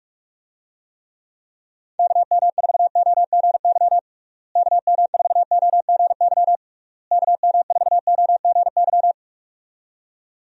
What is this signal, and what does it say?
Morse code, heard as a keyed tone, about 41 wpm, 700 Hz. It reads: KM4OGY KM4OGY KM4OGY
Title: USA Call Signs - 6 Characters - Repeated 3x - 45wpm